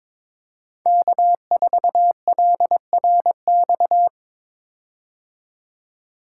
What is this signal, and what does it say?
Morse code, heard as a keyed tone, about 22 wpm, 700 Hz. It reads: K4LRX
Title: K1USN SST QSO Practice - 22wpm